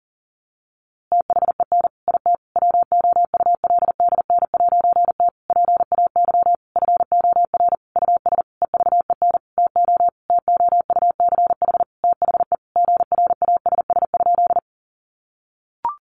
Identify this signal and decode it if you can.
Morse code, heard as a keyed tone, about 40 wpm, 700 Hz. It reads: THEN IT WOULDN'T PAY FOR US EVEN TO TOUCH THE GRASS?
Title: A Sound Of Thunder 40wpm